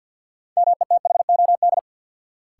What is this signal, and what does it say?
Morse code, heard as a keyed tone, about 49 wpm, 700 Hz. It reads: METHOD
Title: Sentences from Top 600 Words - New Word Drill - 50wpm